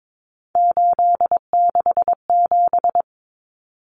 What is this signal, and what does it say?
Morse code, heard as a keyed tone, about 22 wpm, 700 Hz. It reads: 867
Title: Common Cut Numbers - 3-Digit 22wpm